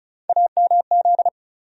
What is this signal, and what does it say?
Morse code, heard as a keyed tone, about 36 wpm, 700 Hz. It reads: AMZ
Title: USA Call Sign Postfixes - 3 Letters 35wpm